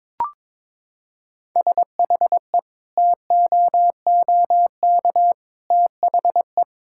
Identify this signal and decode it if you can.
Morse code, heard as a keyed tone, about 21 wpm, 700 Hz. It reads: SHE TOOK THE
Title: Sentences from Top 1000 Words - Review Part 6 - 22wpm